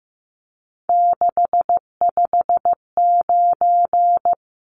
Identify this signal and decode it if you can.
Morse code, heard as a keyed tone, about 15 wpm, 700 Hz. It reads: 659
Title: Numbers - 3-Digit 15wpm